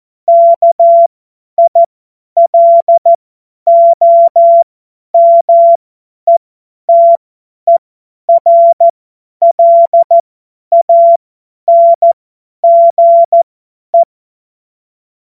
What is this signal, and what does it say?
Morse code, heard as a keyed tone, about 14 wpm, spaced as for 10 wpm, 680 Hz. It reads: KILOMETERLANGE
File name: Morsekurs / Witze / Witz 20 Februar 2024 50 F70